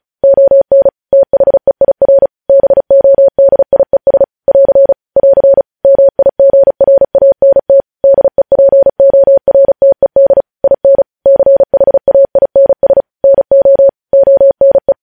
Morse code, heard as a keyed tone, about 36 wpm, 550 Hz. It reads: ON THEIR BODIES <AR> <AR> MIGRANT DEPORTED IN CHAINS NO ONE
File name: podcast / News headlines-1x-35x35